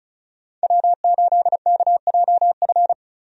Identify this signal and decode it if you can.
Morse code, heard as a keyed tone, about 36 wpm, 700 Hz. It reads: W8KJF
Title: USA Call Signs - 5 Characters - Repeated 3x - 35wpm